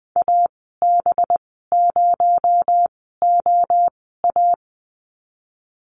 Morse code, heard as a keyed tone, about 20 wpm, spaced as for 14 wpm, 700 Hz. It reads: AB0OA
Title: 231 Lesson 37g - Letter x - Call Signs 20wpm